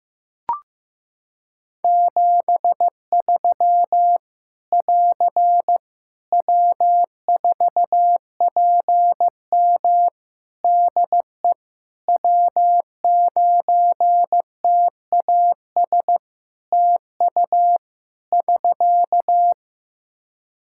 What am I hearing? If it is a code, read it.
Morse code, heard as a keyed tone, about 15 wpm, 700 Hz. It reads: 73 <AR> W4PM DE W9TAS TU <SK>
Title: K1USN SST QSO Practice 15wpm